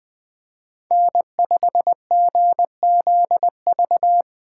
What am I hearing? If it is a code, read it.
Morse code, heard as a keyed tone, about 24 wpm, 700 Hz. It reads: N5GZV
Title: Speed Racing - USA Call Signs 20wpm